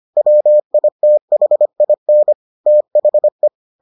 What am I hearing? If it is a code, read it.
Morse code, heard as a keyed tone, about 25 wpm, 600 Hz. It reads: WITHIN THE